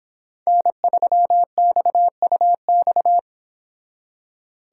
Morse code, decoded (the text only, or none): N3XUX